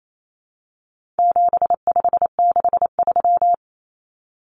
Morse code, heard as a keyed tone, about 28 wpm, 700 Hz. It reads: 7563
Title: Numbers - 4-Digit 28wpm